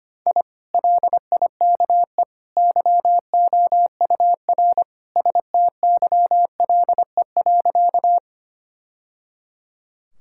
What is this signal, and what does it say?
Morse code, heard as a keyed tone, about 25 wpm, 700 Hz. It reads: I LIKE YOUR STYLE.
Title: Sentences from Top 1000 Words - Review Part 5 - 25wpm